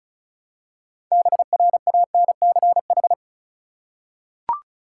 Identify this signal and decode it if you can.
Morse code, heard as a keyed tone, about 36 wpm, 700 Hz. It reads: BRANCH
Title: Six Letter Words - 35wpm